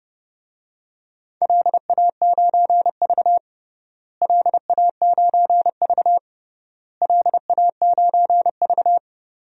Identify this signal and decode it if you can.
Morse code, heard as a keyed tone, about 32 wpm, 700 Hz. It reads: LA9V LA9V LA9V
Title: DXCC-100 Call Signs - 4 Characters - Repeated 3x - 30wpm